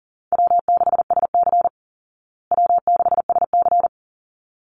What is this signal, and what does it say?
Morse code, heard as a keyed tone, about 40 wpm, 700 Hz. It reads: W6SC W6SC
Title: Easy Call Signs Repeated 2x - 40wpm